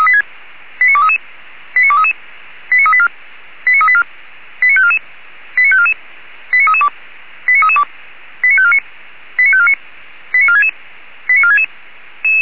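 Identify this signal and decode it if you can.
Five-tone selective call (ZVEI), heard as quick runs of five tones, tones of about 70 ms. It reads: 89233 89233 89384 89384 89544 89544 89332 89332 89549 89549 89466 89466
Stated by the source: ZVEI five-tone selective call received on SDR, from feuerwehr-orlamuende fire brigade server.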